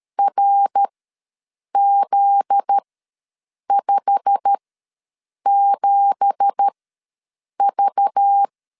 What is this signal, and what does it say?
Morse code, heard as a keyed tone, about 13 wpm, spaced as for 7 wpm, 800 Hz. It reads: RZ57V
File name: LEZIONI / 9 2867 ws 12wpm 3431